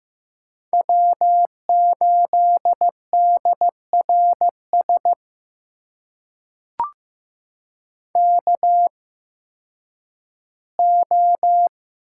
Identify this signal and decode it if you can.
Morse code, heard as a keyed tone, about 15 wpm, spaced as for 10 wpm, 700 Hz. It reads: W8DRS KO
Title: USA Call Signs - 8x Character Spacing - 15wpm